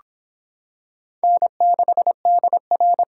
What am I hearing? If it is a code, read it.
Morse code, heard as a keyed tone, about 26 wpm, 700 Hz. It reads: N6DR